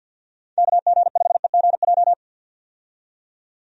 Morse code, heard as a keyed tone, about 52 wpm, 700 Hz. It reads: KG5EGJ
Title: USA Call Signs - 6 Characters - 50wpm